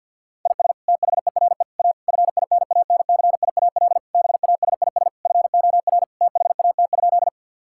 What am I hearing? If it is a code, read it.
Morse code, heard as a keyed tone, about 51 wpm, 690 Hz. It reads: IS THERE A FINANCIAL BASIS FOR THAT?
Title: Sentences from Top 1000 Words - Review Part 5 - 50wpm